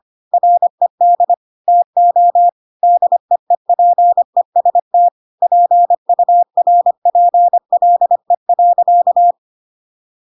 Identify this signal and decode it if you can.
Morse code, heard as a keyed tone, about 25 wpm, 700 Hz. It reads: RED TO DEEPEST PURPLE.
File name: Pinocchio / Pinocchio ch1 25 25 ik1hge